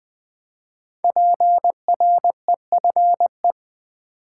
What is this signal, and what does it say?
Morse code, heard as a keyed tone, about 20 wpm, 700 Hz. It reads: PREFE
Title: Build A Word - 5 Letters At A Time - 10-Letter Words - 20wpm